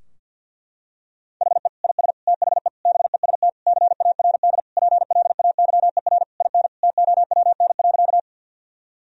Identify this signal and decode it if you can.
Morse code, heard as a keyed tone, about 51 wpm, 700 Hz. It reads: HE IS THE BEST CARD PLAYER IN TOWN.
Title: Sentences from Top 1000 Words - Review Part 4 - 50wpm